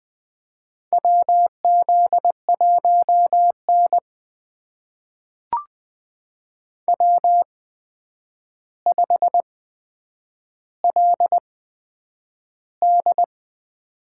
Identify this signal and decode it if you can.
Morse code, heard as a keyed tone, about 20 wpm, spaced as for 9 wpm, 700 Hz. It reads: WZ1N W5LD
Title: USA Call Signs - 4 Characters - 8x Character Spacing - 20wpm